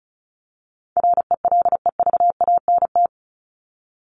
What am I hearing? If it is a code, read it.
Morse code, heard as a keyed tone, about 34 wpm, 700 Hz. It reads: RELEVANT
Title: Top 1001-1250 Words 35wpm